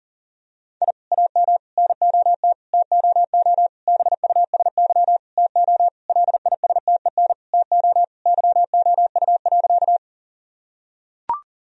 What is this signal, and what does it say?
Morse code, heard as a keyed tone, about 37 wpm, 700 Hz. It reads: I AM NOT TOO BUSY TO LISTEN TO YOU.
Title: Sentences from Top 1000 Words - Review Part 3 - 40wpm